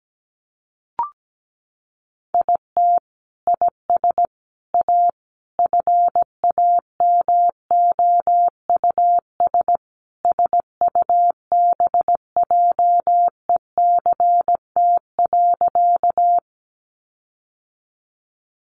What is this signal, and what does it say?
Morse code, heard as a keyed tone, about 17 wpm, 700 Hz. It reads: IT IS A FAMOUS SUBJECT.
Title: Sentences from Top 1500 Words 17wpm